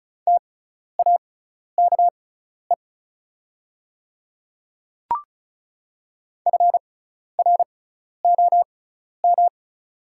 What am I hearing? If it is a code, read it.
Morse code, heard as a keyed tone, about 36 wpm, spaced as for 12 wpm, 690 Hz. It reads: TAKE FROM
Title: 6x Character Spacing - Top 300 Words - 35wpm